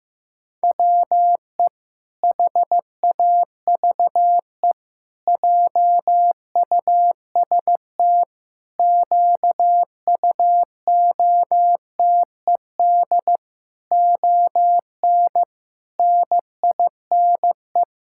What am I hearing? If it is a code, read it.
Morse code, heard as a keyed tone, about 15 wpm, 700 Hz. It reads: WE HAVE JUST QUOTED ON NINE
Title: Pangram Challenge v2 - 15wpm